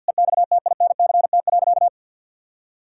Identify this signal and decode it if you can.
Morse code, heard as a keyed tone, about 50 wpm, 700 Hz. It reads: EXTINCT.
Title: Pangram Challenge v2 - 50wpm